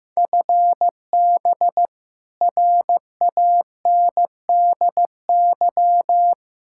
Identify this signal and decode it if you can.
Morse code, heard as a keyed tone, about 15 wpm, 700 Hz. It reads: FB RANDY
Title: K1USN SST QSO Practice 15wpm